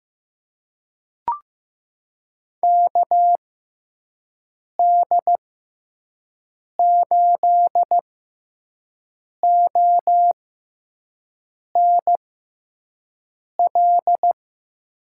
Morse code, heard as a keyed tone, about 16 wpm, spaced as for 5 wpm, 700 Hz. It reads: KD8ONL
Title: USA Call Signs - 6 Characters - 6x Character Spacing - 15wpm